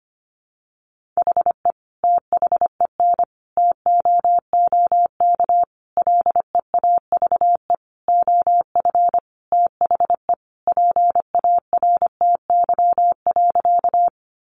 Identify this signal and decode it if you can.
Morse code, heard as a keyed tone, about 25 wpm, 700 Hz. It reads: HE THEN TOOK LEAVE OF THE PARTY.
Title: Sentences from Top 1400 Words 25wpm